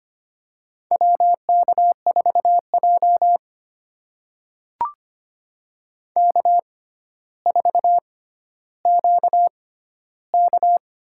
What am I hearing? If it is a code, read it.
Morse code, heard as a keyed tone, about 25 wpm, spaced as for 13 wpm, 700 Hz. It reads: WK4J K4QK
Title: USA Call Signs - 4 Characters - 6x Character Spacing - 25wpm